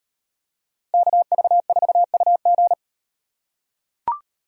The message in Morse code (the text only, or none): KV4UG